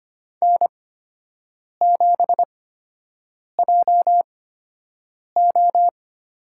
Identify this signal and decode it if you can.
Morse code, heard as a keyed tone, about 25 wpm, spaced as for 7 wpm, 700 Hz. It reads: N7JO
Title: USA Call Signs - 8x Character Spacing - 25wpm